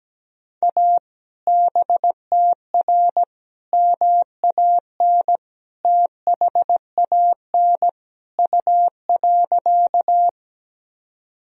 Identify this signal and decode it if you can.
Morse code, heard as a keyed tone, about 17 wpm, 700 Hz. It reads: A BTR MAN THAN U.
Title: Sentences from Top 400 Words and Abbreviations 17wpm